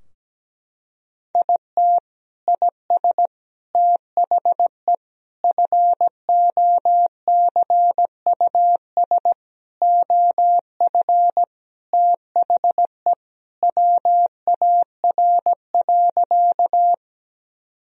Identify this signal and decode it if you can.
Morse code, heard as a keyed tone, about 17 wpm, 700 Hz. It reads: IT IS THE FOCUS OF THE WAR.
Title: Sentences from Top 1000 Words - Review Part 4 - 17wpm